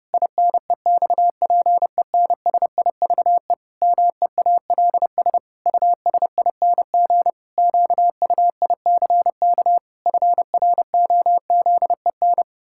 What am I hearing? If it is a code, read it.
Morse code, heard as a keyed tone, about 30 wpm, 700 Hz. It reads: INEXPENSIVE MEALS USING QUICK FROZEN
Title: Pangram Challenge v2 - 30wpm